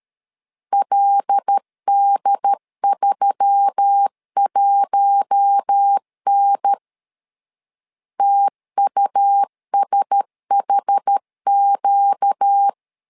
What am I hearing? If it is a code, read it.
Morse code, heard as a keyed tone, about 13 wpm, 800 Hz. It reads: LD31N TUSHQ
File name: LEZIONI / 5 LBVC / 5 L 12wpm 1913